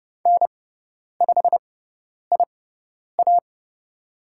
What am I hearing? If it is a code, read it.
Morse code, heard as a keyed tone, about 29 wpm, spaced as for 10 wpm, 700 Hz. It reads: N5IA
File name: Lesson 17g - Letter u - Call Signs 30wpm